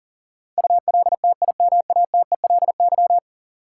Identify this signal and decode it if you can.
Morse code, heard as a keyed tone, about 39 wpm, 700 Hz. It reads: ULTIMATELY
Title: Sentences from Top 1200 Words - New Word Drill Alt - 40wpm